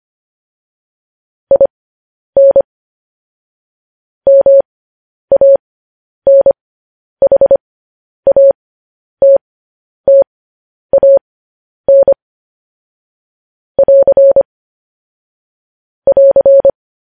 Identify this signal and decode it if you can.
Morse code, heard as a keyed tone, about 25 wpm, spaced as for 10 wpm, 550 Hz. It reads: IN MANHATTAN <AR> <AR>